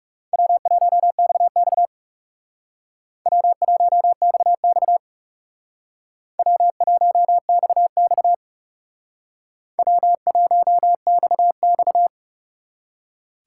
Morse code, heard as a keyed tone, about 38 wpm, 700 Hz. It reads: W1XX W1XX W1XX W1XX